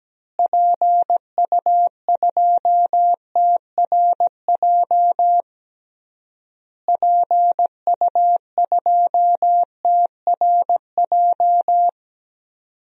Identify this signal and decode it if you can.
Morse code, heard as a keyed tone, about 17 wpm, 700 Hz. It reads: PU2TRJ PU2TRJ
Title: DXCC-100 Call Signs - 6 Characters - Repeated 3x - 17wpm